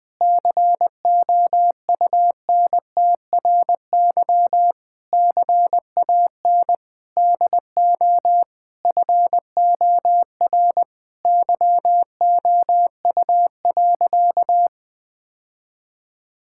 Morse code, heard as a keyed tone, about 21 wpm, 700 Hz. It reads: COUNTRY CAN DO FOR YOU.
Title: Sentences with 8-Word Phrases - 20wpm